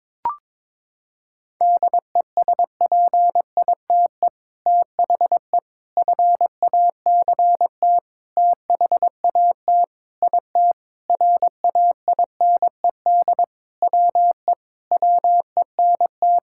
Morse code, heard as a keyed tone, about 22 wpm, 700 Hz. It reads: DESPITE THE FACT THAT IT RAINED WE WENT
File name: practice / 22wpm / Sentences with 4-Word Phrases - 22wpm